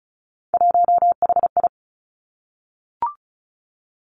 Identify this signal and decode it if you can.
Morse code, heard as a keyed tone, about 36 wpm, 700 Hz. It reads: 1HI
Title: Three Challenging Letter-Number Combinations 35wpm